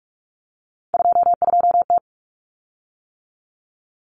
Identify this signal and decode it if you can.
Morse code, heard as a keyed tone, about 44 wpm, 700 Hz. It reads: 22T
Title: Common Cut Numbers - 3-Digit 45wpm